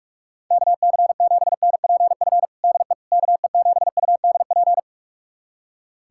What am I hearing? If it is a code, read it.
Morse code, heard as a keyed tone, about 46 wpm, 700 Hz. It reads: KC7NPF DE KE7UDP
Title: USA Callsign DE Callsign - 6 Characters - 45wpm